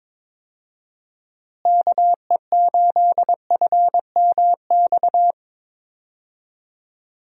Morse code, heard as a keyed tone, about 22 wpm, 700 Hz. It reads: KE8FMX
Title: USA Call Signs - 6 Characters - Repeated 3x - 22wpm